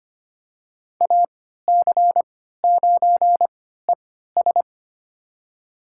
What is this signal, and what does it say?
Morse code, heard as a keyed tone, about 25 wpm, spaced as for 14 wpm, 700 Hz. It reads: AC9ES
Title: USA Call Signs - 3x Character Spacing - 25wpm